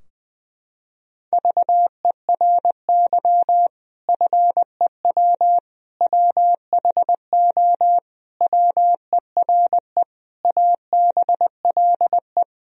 Morse code, heard as a keyed tone, about 21 wpm, 700 Hz. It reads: VERY FEW WHO WERE ABLE
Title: Sentences from Top 1000 Words - Review Part 2 - 20wpm